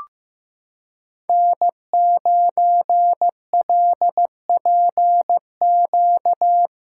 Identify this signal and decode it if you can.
Morse code, heard as a keyed tone, about 15 wpm, 700 Hz. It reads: N9LPQ
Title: USA Callsign DE Callsign 15wpm